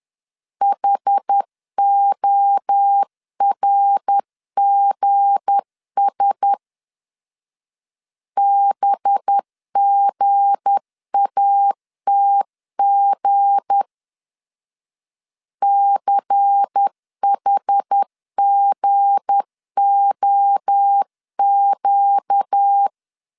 Morse code, heard as a keyed tone, about 10 wpm, 800 Hz. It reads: HORGS BGATG CHGOQ
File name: LEZIONI / 7 GF9Z / 7 G 10wpm 2501